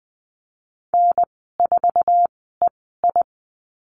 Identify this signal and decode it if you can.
Morse code, heard as a keyed tone, about 20 wpm, spaced as for 14 wpm, 700 Hz. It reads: N4EI
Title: 048 Lesson 9g - Number 4 - Call Signs 20wpm